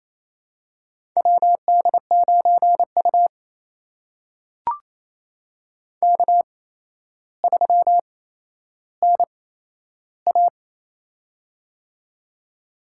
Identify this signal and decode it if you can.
Morse code, heard as a keyed tone, about 27 wpm, spaced as for 12 wpm, 700 Hz. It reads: WD9U K3NA
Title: USA Call Signs - 4 Characters - 8x Character Spacing - 28wpm